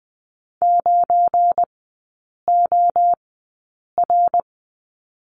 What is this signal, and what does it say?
Morse code, heard as a keyed tone, about 19 wpm, 700 Hz. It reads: 9 O R
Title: 205 Lesson 33e - Letter j - Three Characters 20wpm